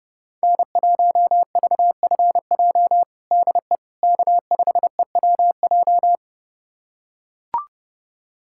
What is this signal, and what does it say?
Morse code, heard as a keyed tone, about 30 wpm, 700 Hz. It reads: N1VFJ DE K5EWJ
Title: USA Callsign DE Callsign - 5 Characters - 30wpm